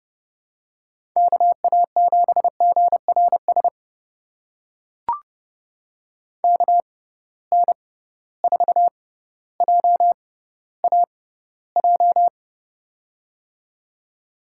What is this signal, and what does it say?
Morse code, heard as a keyed tone, about 30 wpm, spaced as for 15 wpm, 700 Hz. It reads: KA7GRS KN4JAJ
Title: USA Call Signs - 6x Character Spacing - 30wpm